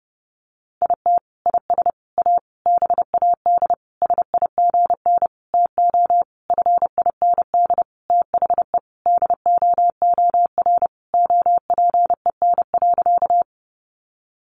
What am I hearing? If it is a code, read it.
Morse code, heard as a keyed tone, about 30 wpm, 700 Hz. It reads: IT IS A BAD SIGN TO FIND THE DOOR OPEN.
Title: Sentences from Top 1000 Words 30wpm